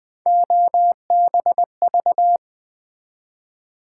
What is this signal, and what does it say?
Morse code, heard as a keyed tone, about 20 wpm, 700 Hz. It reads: OBV
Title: Build A Word - 3 Letters At A Time - 9-Letter Words - 20wpm